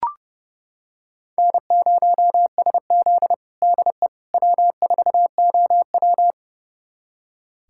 Morse code, heard as a keyed tone, about 30 wpm, 700 Hz. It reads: N0SZ DE W4OW
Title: USA Callsign DE Callsign - 4 Characters - 30wpm